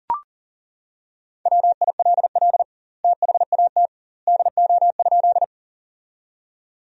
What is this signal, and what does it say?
Morse code, heard as a keyed tone, about 40 wpm, 700 Hz. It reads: WILL THAT DO?
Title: 2x Word-Spacing - Sentences from Top 100 Words - 40wpm